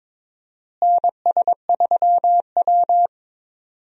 Morse code, heard as a keyed tone, about 22 wpm, 700 Hz. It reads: NS3W
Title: USA Call Signs - 4 Characters - Repeated 3x - 22wpm